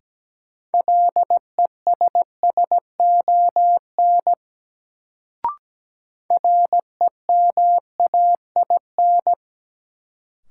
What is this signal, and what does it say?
Morse code, heard as a keyed tone, about 17 wpm, 700 Hz. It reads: LESSON REMAIN
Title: Sentences from Top 1200 Words - New Word Drill Alt - 17wpm